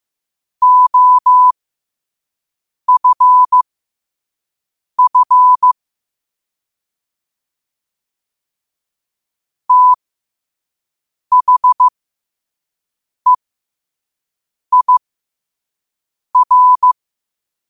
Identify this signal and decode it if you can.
Morse code, heard as a keyed tone, about 15 wpm, spaced as for 5 wpm, 1000 Hz. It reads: OFF THEIR
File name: qotdfiles / 5wpm